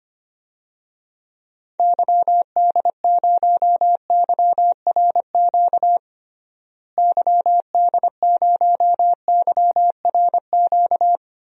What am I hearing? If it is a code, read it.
Morse code, heard as a keyed tone, about 25 wpm, 700 Hz. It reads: YD0YRQ YD0YRQ
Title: DXCC-100 Call Signs - 6 Characters - Repeated 3x - 25wpm